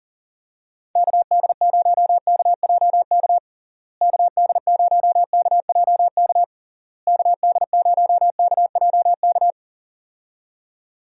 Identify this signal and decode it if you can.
Morse code, heard as a keyed tone, about 40 wpm, 700 Hz. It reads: KD0KJK KD0KJK KD0KJK
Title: USA Call Signs - 6 Characters - Repeated 3x - 40wpm